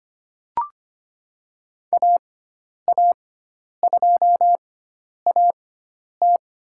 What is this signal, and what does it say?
Morse code, heard as a keyed tone, about 25 wpm, spaced as for 10 wpm, 700 Hz. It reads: AA2AT
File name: Lesson 16g - Letter c - Call Signs 25wpm